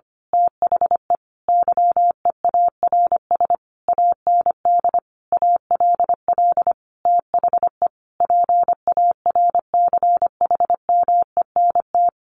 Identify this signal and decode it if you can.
Morse code, heard as a keyed tone, about 25 wpm, 700 Hz. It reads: THE YEARS AND ALL THE PARCHMENT
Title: A Sound Of Thunder 25wpm